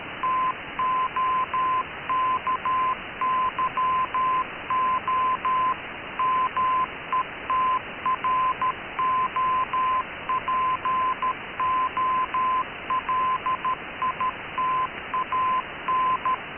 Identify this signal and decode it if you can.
Morse code, heard as a keyed tone, about 13 wpm, 1030 Hz. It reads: TOKYO METROPOLITAN
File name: KKS-1 cw 21jul2014 1425z